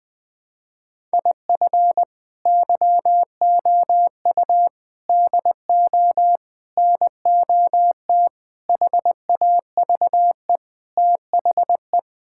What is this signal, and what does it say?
Morse code, heard as a keyed tone, about 20 wpm, 700 Hz. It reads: IF YOU DO NOT HAVE THE
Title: Sentences with 5-Word Phrases - Spoken Beginning - 20wpm